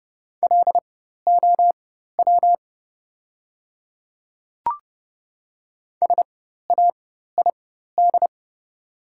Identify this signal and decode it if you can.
Morse code, heard as a keyed tone, about 30 wpm, spaced as for 14 wpm, 700 Hz. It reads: LOW SAID
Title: 4x Character Spacing - Top 200 Words - 30wpm